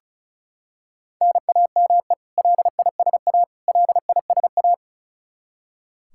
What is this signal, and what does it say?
Morse code, heard as a keyed tone, about 36 wpm, 700 Hz. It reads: NAME LISA LISA
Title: Practice QSO - Exchange 1 - Beginning - 35wpm